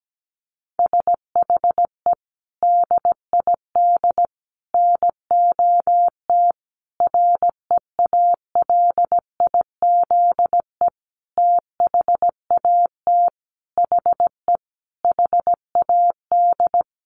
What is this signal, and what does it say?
Morse code, heard as a keyed tone, about 17 wpm, 700 Hz. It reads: SHE DID NOT REALIZE THAT HE HAD
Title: Sentences from Top 1000 Words 17wpm